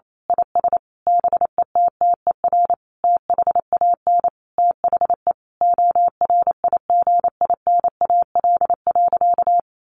Morse code, heard as a keyed tone, about 28 wpm, 700 Hz. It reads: IS BETTER THAN THE ORIGINAL.
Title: Sentences from Top 1400 Words 28wpm